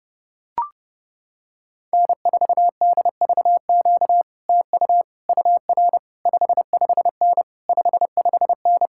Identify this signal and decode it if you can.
Morse code, heard as a keyed tone, about 30 wpm, 700 Hz. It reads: N4DVQ TU UR 55N 55N